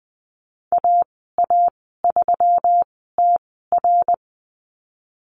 Morse code, heard as a keyed tone, about 19 wpm, spaced as for 14 wpm, 700 Hz. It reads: AA3TR
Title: 179 Lesson 29g - Slash - Call Signs 20wpm